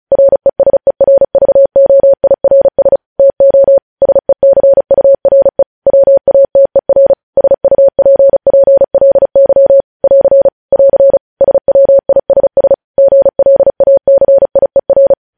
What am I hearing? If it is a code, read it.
Morse code, heard as a keyed tone, about 36 wpm, 550 Hz. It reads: RESERVOIRS TO SECURE WATER SUPPLY <AR> <AR> SWISS GLACIER
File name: podcast / News headlines-1x-35x35